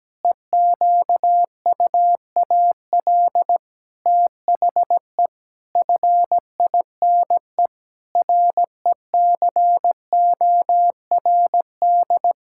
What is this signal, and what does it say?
Morse code, heard as a keyed tone, about 17 wpm, 700 Hz. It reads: EQUAL THE FINE RECORD
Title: Pangram Challenge v2 - 17wpm